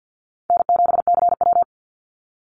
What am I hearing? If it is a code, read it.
Morse code, heard as a keyed tone, about 50 wpm, 700 Hz. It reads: N7CW
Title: NA SOTA Chaser Callsigns 50wpm